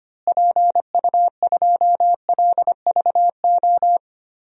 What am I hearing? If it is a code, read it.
Morse code, heard as a keyed tone, about 25 wpm, 700 Hz. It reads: PU2LVO
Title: DXCC-100 Call Signs - 6 Characters - Repeated 3x - 25wpm